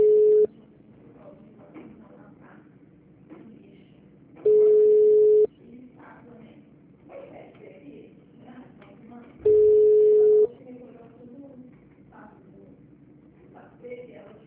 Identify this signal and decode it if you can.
Ringback tone (Europe, 425 Hz), with speech.
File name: out- -2003- - - .86